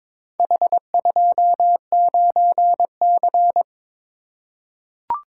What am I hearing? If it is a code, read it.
Morse code, heard as a keyed tone, about 22 wpm, 700 Hz. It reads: H29C
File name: practice / 22wpm / Four Letter-Number Combinations - 22wpm